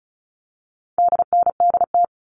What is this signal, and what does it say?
Morse code, heard as a keyed tone, about 34 wpm, 700 Hz. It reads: DNDT
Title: All Cut Numbers - 4-Digit 35wpm